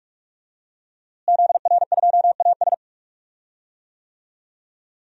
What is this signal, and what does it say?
Morse code, heard as a keyed tone, about 43 wpm, 700 Hz. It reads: ZR2AS
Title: DXCC-100 Call Signs - 5 Characters - Repeated 3x - 45wpm